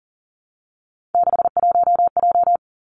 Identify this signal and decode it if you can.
Morse code, heard as a keyed tone, about 39 wpm, 700 Hz. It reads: 61J
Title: Three Challenging Letter-Number Combinations 40wpm